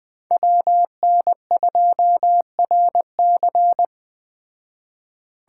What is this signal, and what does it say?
Morse code, heard as a keyed tone, about 20 wpm, 700 Hz. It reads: WN2RC